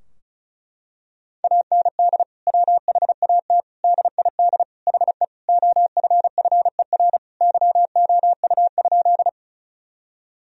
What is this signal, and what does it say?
Morse code, heard as a keyed tone, about 34 wpm, 700 Hz. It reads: AND WHAT DID HE OFFER YOU?
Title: Sentences from Top 1000 Words - Review Part 4 - 35wpm